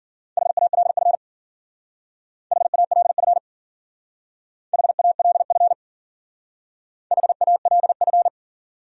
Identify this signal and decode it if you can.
Morse code, heard as a keyed tone, about 52 wpm, 700 Hz. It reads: HALF HALF HALF HALF